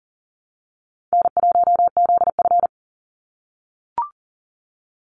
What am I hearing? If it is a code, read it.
Morse code, heard as a keyed tone, about 40 wpm, 700 Hz. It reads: N1ZF
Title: NA SOTA Chaser Callsigns 40wpm